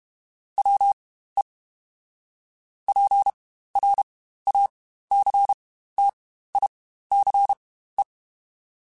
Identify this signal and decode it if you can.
Morse code, heard as a keyed tone, about 32 wpm, spaced as for 15 wpm, 800 Hz. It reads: WE PRACTICE
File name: Quotes Spaced / Quotes Spaced-mp3s / Deceive sp @30